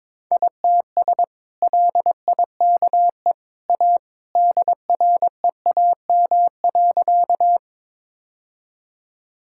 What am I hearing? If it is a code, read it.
Morse code, heard as a keyed tone, about 22 wpm, 700 Hz. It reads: ITS LIKE A DREAM.
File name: practice / 22wpm / Sentences with 2-Word Phrases - 22wpm